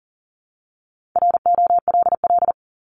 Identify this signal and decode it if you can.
Morse code, heard as a keyed tone, about 40 wpm, 700 Hz. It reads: ROLL
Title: Top 1751-2000 Words 40wpm